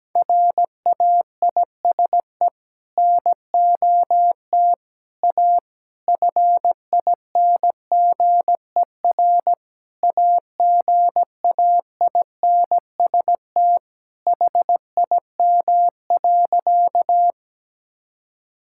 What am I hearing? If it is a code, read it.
Morse code, heard as a keyed tone, about 17 wpm, 700 Hz. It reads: RAISE NOT A FINGER AGAINST HIM.
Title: Sentences from Top 1600 Words 17wpm